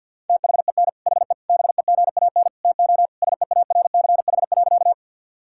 Morse code, heard as a keyed tone, about 49 wpm, 690 Hz. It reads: THEN HE BEGAN TO SEARCH.
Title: Sentences from Top 1000 Words - Review Part 5 - 50wpm